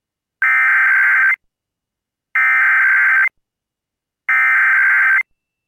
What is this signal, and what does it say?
Emergency Alert System (SAME) tones (header bursts), heard as harsh buzzing bursts. It is heard from 0.4 to 5.2 s.